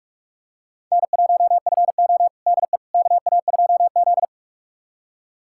Morse code, heard as a keyed tone, about 43 wpm, 700 Hz. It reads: N1FO DE KA2Z
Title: USA Callsign DE Callsign - 4 Characters - 45wpm